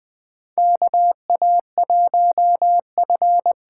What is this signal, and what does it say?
Morse code, heard as a keyed tone, about 20 wpm, 700 Hz. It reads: KA1F
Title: USA Call Signs - 6 Characters - 8x Character Spacing - 20wpm